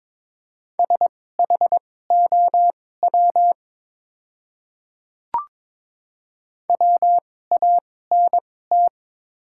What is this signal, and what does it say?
Morse code, heard as a keyed tone, about 22 wpm, spaced as for 16 wpm, 700 Hz. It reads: SHOW WANT
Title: 2x Character Spacing - Top 200 Words - 22wpm